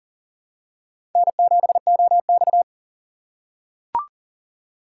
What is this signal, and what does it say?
Morse code, heard as a keyed tone, about 40 wpm, 700 Hz. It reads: N7OX
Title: USA Call Signs - 4 Characters - 2x Character Spacing - 40wpm